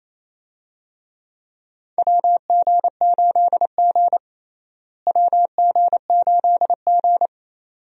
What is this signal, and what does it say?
Morse code, heard as a keyed tone, about 28 wpm, 700 Hz. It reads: WG8G WG8G
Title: USA Call Signs - 4 Characters - Repeated 2x - 28wpm